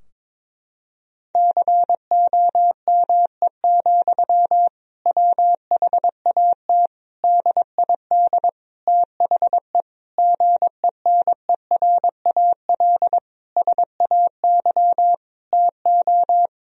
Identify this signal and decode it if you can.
Morse code, heard as a keyed tone, about 22 wpm, 700 Hz. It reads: COME, WHAT DID THE GENERAL SAY TO
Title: Sentences from Top 1000 Words - Review Part 3 - 22wpm